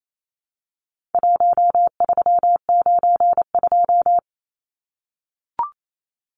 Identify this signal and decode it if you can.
Morse code, heard as a keyed tone, about 28 wpm, 700 Hz. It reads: 1392
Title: Numbers - Contest Style - 4-Digit 28wpm